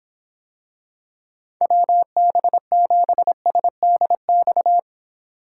Morse code, heard as a keyed tone, about 26 wpm, 700 Hz. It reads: WB7SDX